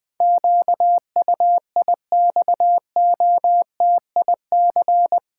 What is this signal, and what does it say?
Morse code, heard as a keyed tone, about 21 wpm, 700 Hz. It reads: QUIXOTIC
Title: Pangram Challenge v2 - 20wpm